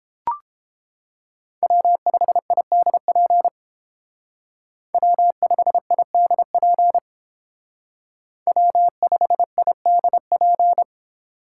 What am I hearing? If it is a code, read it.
Morse code, heard as a keyed tone, about 31 wpm, 700 Hz. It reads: W5IDP W5IDP W5IDP